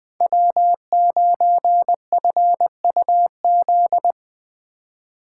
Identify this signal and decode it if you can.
Morse code, heard as a keyed tone, about 20 wpm, 700 Hz. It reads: W9FUZ
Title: USA Call Signs - 5 Characters - 2x Character Spacing - 20wpm